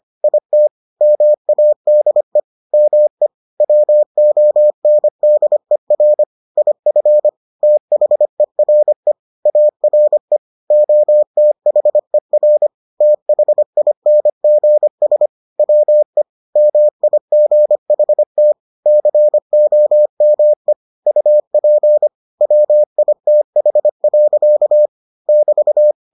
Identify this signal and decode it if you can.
Morse code, heard as a keyed tone, about 25 wpm, 600 Hz. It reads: IT MADE ME WONDER IF THERE ARE OTHER THINGS WE MIGHT COME UP WITH. <BT>